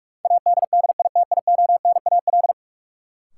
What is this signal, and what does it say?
Morse code, heard as a keyed tone, about 46 wpm, 700 Hz. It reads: ADDITIONAL
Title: Sentences from Top 1400 Words - New Word Drill Alt - 45wpm